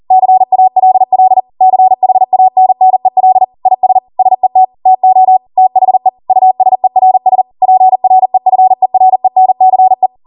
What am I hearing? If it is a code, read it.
Morse code, heard as a keyed tone, about 40 wpm, 750 Hz. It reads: CALL CHANNEL IS SET TO THE USERS PREFERENCE
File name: Morse / 40WPM